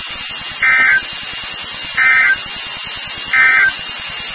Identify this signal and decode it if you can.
Emergency Alert System (SAME) tones, heard as harsh buzzing bursts. It is heard from 0.5 to 3.8 s.
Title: NOAA Weather Radio WXK36